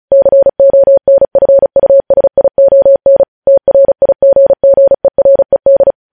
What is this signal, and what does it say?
Morse code, heard as a keyed tone, about 36 wpm, 550 Hz. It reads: CONFUSION TRIGGERED
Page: news headlines in morse code at 35 wpm